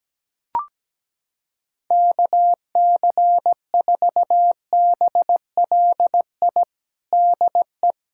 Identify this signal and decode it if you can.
Morse code, heard as a keyed tone, about 17 wpm, 700 Hz. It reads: KC4BLI DE
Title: USA Callsign DE Callsign 17wpm